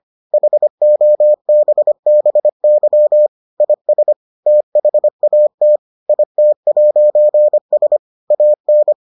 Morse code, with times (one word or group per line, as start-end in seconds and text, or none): 0.3-3.3 HOBBY
3.6-4.1 IS
4.5-5.8 THAT
6.1-8.0 IT'S
8.3-8.9 AN